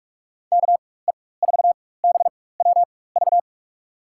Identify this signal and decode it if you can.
Morse code, heard as a keyed tone, about 41 wpm, spaced as for 20 wpm, 700 Hz. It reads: KE4BWV
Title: USA Call Signs - 6 Characters - 4x Character Spacing - 45wpm